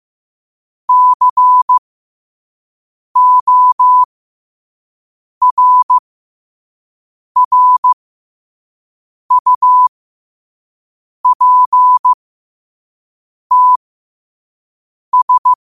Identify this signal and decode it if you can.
Morse code, heard as a keyed tone, about 15 wpm, spaced as for 5 wpm, 1000 Hz. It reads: CORRUPTS
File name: qotdfiles / 5wpm